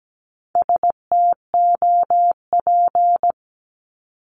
Morse code, heard as a keyed tone, about 17 wpm, 700 Hz. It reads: STOP
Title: Top 500 Words 17wpm